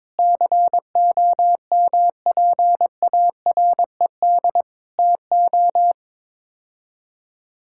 Morse code, heard as a keyed tone, about 22 wpm, 700 Hz. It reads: COMPARED TO
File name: practice / 22wpm / Sets of 7 Words - 22wpm